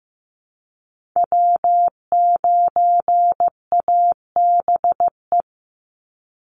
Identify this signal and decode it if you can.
Morse code, heard as a keyed tone, about 15 wpm, 700 Hz. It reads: W9ABE
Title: USA Call Signs 15wpm